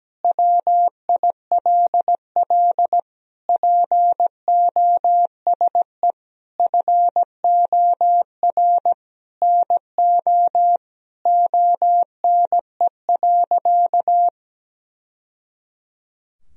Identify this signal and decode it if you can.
Morse code, heard as a keyed tone, about 17 wpm, 700 Hz. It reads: WILL POSE FOR NO ONE.
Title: Sentences from Top 1000 Words - Review Part 2 - 17wpm